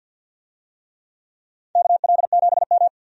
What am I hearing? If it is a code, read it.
Morse code, heard as a keyed tone, about 52 wpm, 700 Hz. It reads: KL7M
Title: DXCC-100 Call Signs - 4 Characters - 50wpm